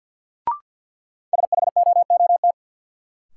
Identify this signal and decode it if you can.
Morse code, heard as a keyed tone, about 53 wpm, 700 Hz. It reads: SHOOT
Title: Sentences from Top 1200 Words - New Word Drill - 50wpm